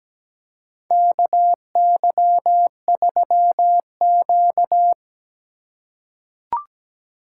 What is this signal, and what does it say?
Morse code, heard as a keyed tone, about 17 wpm, 700 Hz. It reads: KY3Q
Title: USA Call Signs - 4 Characters - 4x Character Spacing - 17wpm